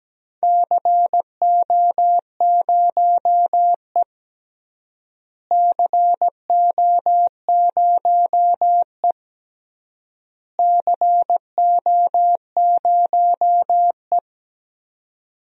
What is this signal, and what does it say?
Morse code, heard as a keyed tone, about 17 wpm, 700 Hz. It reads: CO0E CO0E CO0E
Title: DXCC-100 Call Signs - 4 Characters - Repeated 3x - 17wpm